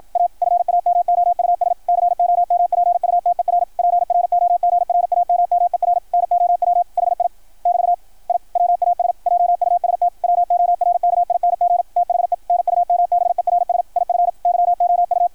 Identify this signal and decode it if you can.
Morse code, heard as a keyed tone, about 53 wpm, 690 Hz. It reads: A FAMOUS COMPUTER PROGRAMMER NOW HI <BT> I WAS JUST WORKING THE NUMBERS IF YOU
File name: archived / CRSnet